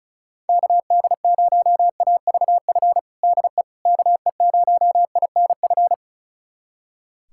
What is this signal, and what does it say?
Morse code, heard as a keyed tone, about 34 wpm, 690 Hz. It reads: KD0AVF DE KE0INF
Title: USA Callsign DE Callsign 35wpm